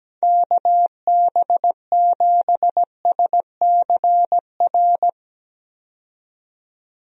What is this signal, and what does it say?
Morse code, heard as a keyed tone, about 17 wpm, 700 Hz. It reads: KB7SCR
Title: USA Call Signs - 6 Characters - 17wpm